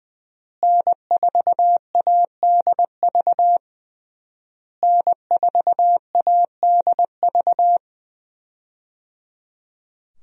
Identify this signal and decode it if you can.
Morse code, heard as a keyed tone, about 20 wpm, 700 Hz. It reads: N4ADV N4ADV
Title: USA Call Signs - 5 Characters - Repeated 3x - 20wpm